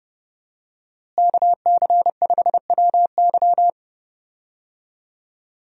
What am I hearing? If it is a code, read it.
Morse code, heard as a keyed tone, about 31 wpm, 700 Hz. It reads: KC5WY
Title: USA Call Signs - 5 Characters - Repeated 3x - 30wpm